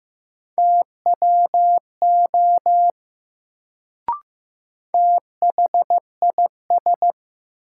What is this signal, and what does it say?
Morse code, heard as a keyed tone, about 15 wpm, 700 Hz. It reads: TWO THIS
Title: Sentences from Top 100 Words - New Word Drill - 15wpm